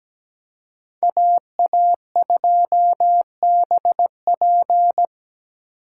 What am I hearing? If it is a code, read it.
Morse code, heard as a keyed tone, about 20 wpm, 700 Hz. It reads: AA2BP
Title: Speed Racing - USA Call Signs 17wpm